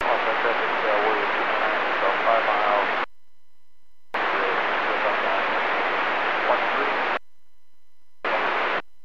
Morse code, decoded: none

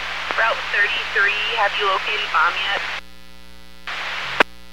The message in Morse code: none

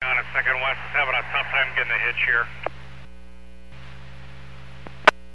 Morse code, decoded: none

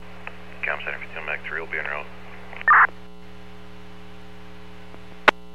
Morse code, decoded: none

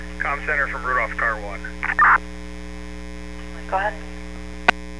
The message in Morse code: none

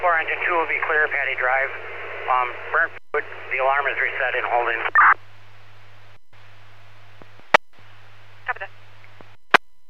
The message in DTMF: none

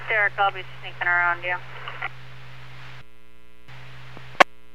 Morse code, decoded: none

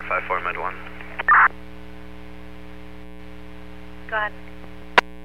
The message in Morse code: none

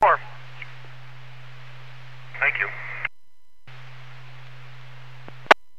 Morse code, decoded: none

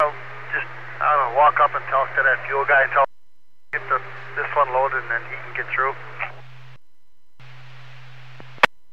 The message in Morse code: none

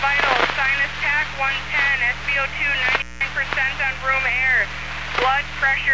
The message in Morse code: none

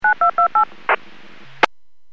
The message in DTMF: 9228

